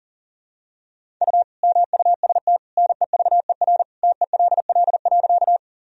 Morse code, heard as a keyed tone, about 40 wpm, 700 Hz. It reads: U MUST NEVER TELL.
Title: Sentences from Top 300 Words and Abbreviations 40wpm